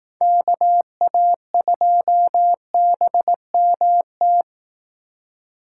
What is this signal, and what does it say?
Morse code, heard as a keyed tone, about 18 wpm, 700 Hz. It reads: KA2BMT
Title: Speed Racing - USA 2x3 Call Signs 15wpm